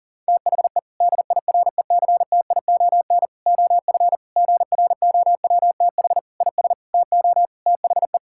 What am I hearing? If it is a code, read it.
Morse code, heard as a keyed tone, about 40 wpm, 700 Hz. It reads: THE DIRECTION OF GROWTH IS TO THE
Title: Sentences from Top 1000 Words - Review Part 5 - 40wpm